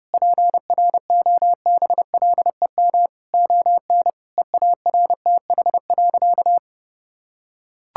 Morse code, heard as a keyed tone, about 32 wpm, 700 Hz. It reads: PROBLEM ON EARTH.
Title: Sentences from Top 1000 Words - Review Part 6 - 30wpm